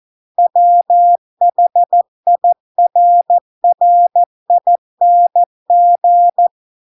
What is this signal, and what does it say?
Morse code, heard as a keyed tone, about 14 wpm, 700 Hz. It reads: WHIRRING